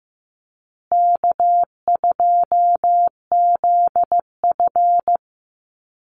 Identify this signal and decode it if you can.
Morse code, heard as a keyed tone, about 15 wpm, 700 Hz. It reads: K2ZF
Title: Easy Call Signs Repeated 3x - 15wpm